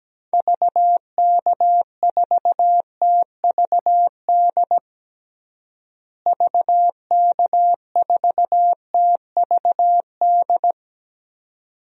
Morse code, heard as a keyed tone, about 17 wpm, 700 Hz. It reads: VK4TVD VK4TVD
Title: DXCC-100 Call Signs - 6 Characters - Repeated 2x - 17wpm